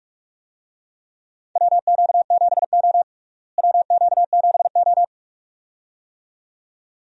Morse code, heard as a keyed tone, about 42 wpm, 700 Hz. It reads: WQ7O WQ7O
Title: USA Call Signs - 4 Characters - Repeated 2x - 45wpm